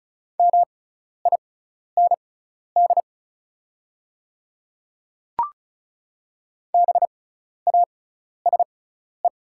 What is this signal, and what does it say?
Morse code, heard as a keyed tone, about 34 wpm, spaced as for 12 wpm, 700 Hz. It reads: MIND BASE
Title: 6x Character Spacing - Top 500 Words - 35wpm